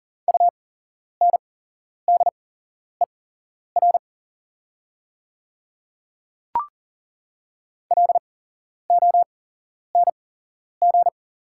Morse code, heard as a keyed tone, about 41 wpm, spaced as for 11 wpm, 690 Hz. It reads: UNDER LONG
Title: 8x Character Spacing - Top 300 Words - 40wpm